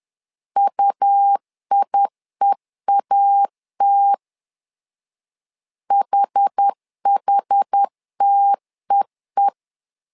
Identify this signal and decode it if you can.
Morse code, heard as a keyed tone, about 10 wpm, 800 Hz. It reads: UIEAT HHTEE
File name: LEZIONI / 2 UHDN / 2 UH 10wpm 1714